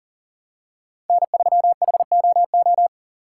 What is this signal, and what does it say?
Morse code, heard as a keyed tone, about 40 wpm, 700 Hz. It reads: N3HOO
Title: USA Call Signs - 5 Characters - 6x Character Spacing - 40wpm